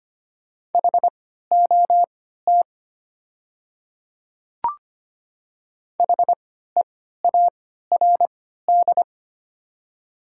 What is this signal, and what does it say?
Morse code, heard as a keyed tone, about 25 wpm, spaced as for 14 wpm, 700 Hz. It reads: HOT HEARD
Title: 3x Character Spacing - Top 500 Words - 25wpm